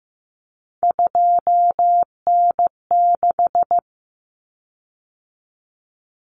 Morse code, heard as a keyed tone, about 15 wpm, 700 Hz. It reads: 2N6
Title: Common Cut Numbers - 3-Digit 15wpm